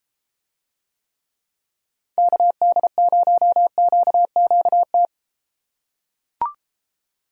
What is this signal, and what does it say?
Morse code, heard as a keyed tone, about 33 wpm, 700 Hz. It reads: KD0QQT